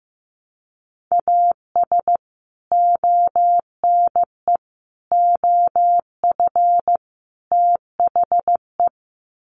Morse code, heard as a keyed tone, about 15 wpm, 700 Hz. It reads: AS ONE OF THE
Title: Sets of 4 Words 15wpm